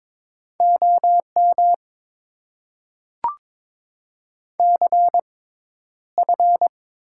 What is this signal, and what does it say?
Morse code, heard as a keyed tone, about 22 wpm, spaced as for 11 wpm, 700 Hz. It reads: OM CF
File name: Top 90 QSO Elements - 6x Character Spacing - 22wpm